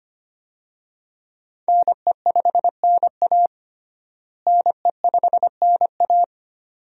Morse code, heard as a keyed tone, about 25 wpm, 700 Hz. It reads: NE5NA NE5NA
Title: USA Call Signs - 5 Characters - Repeated 2x - 25wpm